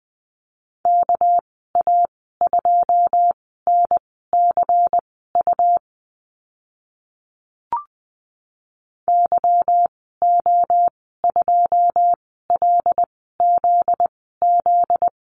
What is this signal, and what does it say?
Morse code, heard as a keyed tone, about 20 wpm, spaced as for 14 wpm, 700 Hz. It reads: KA2NCU YO2LZZ
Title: 245 Lesson 39g - Letter z - Call Signs 20wpm